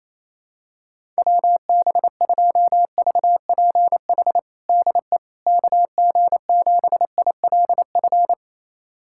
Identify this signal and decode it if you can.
Morse code, heard as a keyed tone, about 28 wpm, 700 Hz. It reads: WB2VPH DE KG7ILF
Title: USA Callsign DE Callsign 28wpm